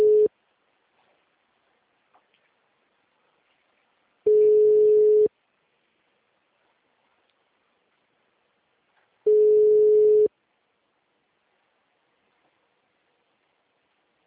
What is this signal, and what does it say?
Ringback tone (Europe, 425 Hz).